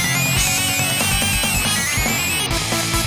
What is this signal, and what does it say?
There is no code.